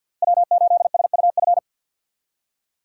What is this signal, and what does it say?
Morse code, heard as a keyed tone, about 47 wpm, 700 Hz. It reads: W8SUF
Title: USA Call Signs - 5 Characters - 4x Character Spacing - 50wpm